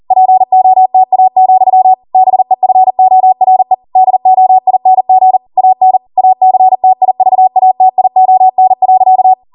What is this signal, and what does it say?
Morse code, heard as a keyed tone, about 40 wpm, 750 Hz. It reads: POTA, BEFORE DOING AN ACTIVATION.